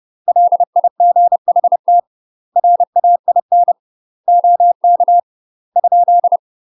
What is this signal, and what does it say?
Morse code, heard as a keyed tone, about 30 wpm, 700 Hz. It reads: LIGHT RAIN OK ?
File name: UK MASTER QSOs TRAINING / 30WPM / UK MASTER QSOs 0004